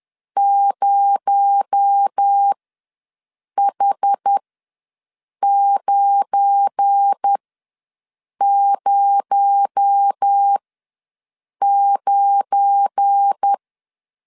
Morse code, heard as a keyed tone, about 10 wpm, spaced as for 6 wpm, 800 Hz. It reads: 0H909
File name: LEZIONI / 7 GF9Z / 7 GF9 ws 10wpm 4034